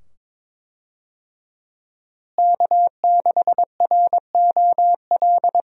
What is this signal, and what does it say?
Morse code, heard as a keyed tone, about 22 wpm, 700 Hz. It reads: K6ROL
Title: USA Call Signs - 5 Characters - Repeated 2x - 22wpm